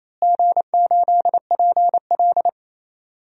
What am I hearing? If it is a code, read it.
Morse code, heard as a keyed tone, about 28 wpm, 700 Hz. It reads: G8PL
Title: DXCC-100 Call Signs - 4 Characters - Repeated 3x - 28wpm